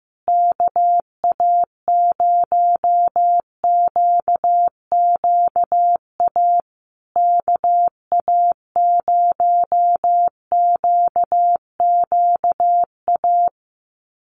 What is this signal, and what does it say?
Morse code, heard as a keyed tone, about 15 wpm, 700 Hz. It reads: KA0QQA KA0QQA
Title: USA Call Signs Repeated 3x 15wpm